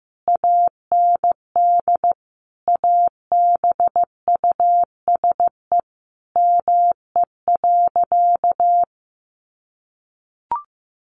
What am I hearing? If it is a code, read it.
Morse code, heard as a keyed tone, about 15 wpm, 700 Hz. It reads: AND ABUSE ME.
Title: Sentences from Top 1300 Words 15wpm